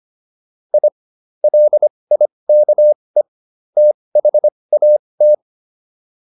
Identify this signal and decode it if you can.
Morse code, heard as a keyed tone, about 25 wpm, spaced as for 20 wpm, 600 Hz. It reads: I LIKE THAT